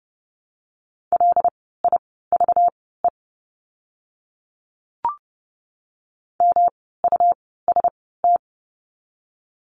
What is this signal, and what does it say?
Morse code, heard as a keyed tone, about 30 wpm, spaced as for 17 wpm, 700 Hz. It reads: LIVE MUST
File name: course / 30:10wpm / 237 Lesson 38f - Letter q - Words 30wpm